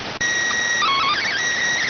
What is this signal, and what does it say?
SSTV (slow-scan television), PD 290.